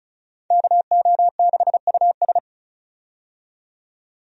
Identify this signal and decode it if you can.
Morse code, heard as a keyed tone, about 34 wpm, 700 Hz. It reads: KO6US
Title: USA Call Signs - 5 Characters - 35wpm